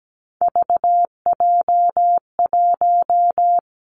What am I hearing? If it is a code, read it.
Morse code, heard as a keyed tone, about 17 wpm, 700 Hz. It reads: VJ1
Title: Three Challenging Letter-Number Combinations 17wpm